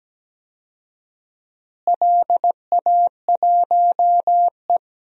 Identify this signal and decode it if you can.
Morse code, heard as a keyed tone, about 17 wpm, 700 Hz. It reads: LA1E
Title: DXCC-100 Call Signs - 4 Characters - Repeated 3x - 17wpm